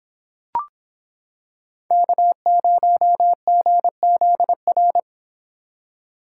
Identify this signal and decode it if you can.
Morse code, heard as a keyed tone, about 26 wpm, 700 Hz. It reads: K0GZR